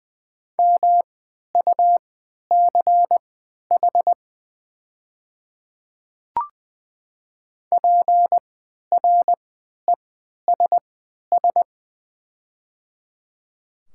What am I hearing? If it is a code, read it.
Morse code, heard as a keyed tone, about 20 wpm, spaced as for 11 wpm, 700 Hz. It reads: MUCH PRESS
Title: 3x Character Spacing - Top 300 Words - 20wpm